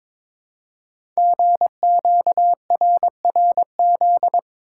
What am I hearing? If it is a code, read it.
Morse code, heard as a keyed tone, about 22 wpm, 700 Hz. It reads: GQRRZ
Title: Code Groups - 5 Letters - 22wpm